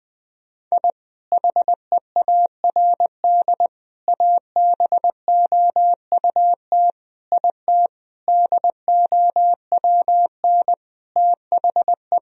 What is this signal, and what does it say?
Morse code, heard as a keyed tone, about 20 wpm, 700 Hz. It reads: I HEARD ABOUT IT DOWN THE
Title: Sentences from Top 1000 Words - Review Part 1 - 20wpm